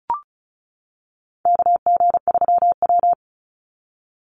Morse code, heard as a keyed tone, about 34 wpm, 700 Hz. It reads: KG3W
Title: NA SOTA Chaser Callsigns 35wpm